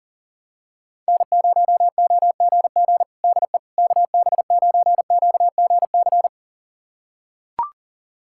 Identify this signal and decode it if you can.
Morse code, heard as a keyed tone, about 40 wpm, 700 Hz. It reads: N0OGG DE KB9QGC
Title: USA Callsign DE Callsign 40wpm